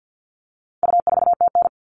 Morse code, heard as a keyed tone, about 50 wpm, 700 Hz. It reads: U4TN
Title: All Cut Numbers - 4-Digit 50wpm